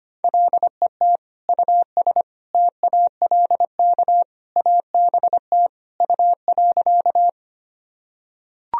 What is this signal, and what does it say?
Morse code, heard as a keyed tone, about 25 wpm, 700 Hz. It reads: LET US TALK ABT U.
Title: Sentences from Top 400 Words and Abbreviations 25wpm